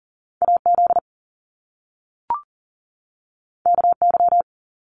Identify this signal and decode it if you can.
Morse code, heard as a keyed tone, about 42 wpm, 700 Hz. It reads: AZ KY
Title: US States 40wpm